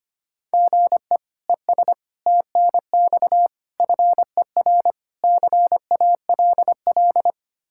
Morse code, heard as a keyed tone, about 25 wpm, 700 Hz. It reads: GE ES TNX FER CALL
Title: K1USN SST QSO Practice 25wpm